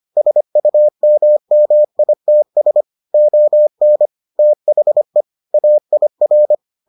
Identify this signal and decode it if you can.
Morse code, heard as a keyed tone, about 25 wpm, 600 Hz. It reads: SUMMITS ON THE AIR